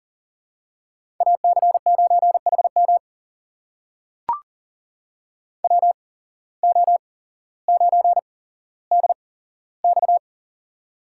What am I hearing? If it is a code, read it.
Morse code, heard as a keyed tone, about 40 wpm, spaced as for 17 wpm, 700 Hz. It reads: AC9HM WO9DX
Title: USA Call Signs - 5 Characters - 8x Character Spacing - 40wpm